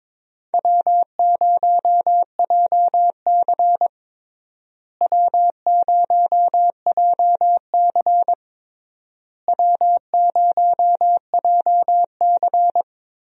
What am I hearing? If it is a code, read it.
Morse code, heard as a keyed tone, about 21 wpm, 700 Hz. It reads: W0JC W0JC W0JC
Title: USA Call Signs - 4 Characters - Repeated 3x - 22wpm